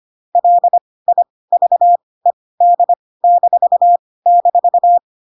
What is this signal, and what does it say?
Morse code, heard as a keyed tone, about 25 wpm, spaced as for 18 wpm, 700 Hz. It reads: LIVED--
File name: Pinocchio / Pinocchio ch1 12 25 ik1hge